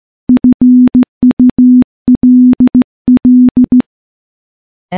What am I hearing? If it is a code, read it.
Morse code, heard as a keyed tone, about 15 wpm, 250 Hz. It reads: FULL
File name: morse / morse-training-15WPM